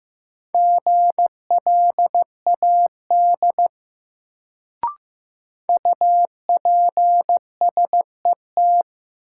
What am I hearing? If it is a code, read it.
Morse code, heard as a keyed tone, about 15 wpm, 700 Hz. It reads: GLAD UPSET
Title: Sentences from Top 800 Words - New Word Drill - 15wpm